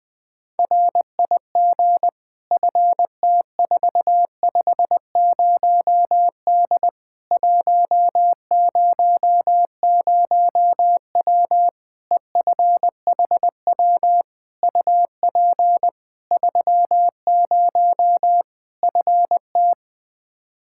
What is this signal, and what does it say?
Morse code, heard as a keyed tone, about 21 wpm, 700 Hz. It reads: RIG FT450D 100W EFHW UP 30 FT
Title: K1USN SST QSO Practice 20wpm